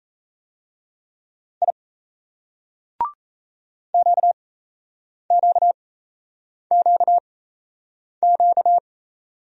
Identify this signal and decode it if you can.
Morse code, heard as a keyed tone, about 38 wpm, spaced as for 8 wpm, 700 Hz. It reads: I QQQQ